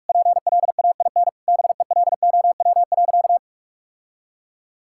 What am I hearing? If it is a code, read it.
Morse code, heard as a keyed tone, about 46 wpm, 700 Hz. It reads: PLAIN BELOW.
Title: Sentences from Top 1000 Words - Review Part 5 - 45wpm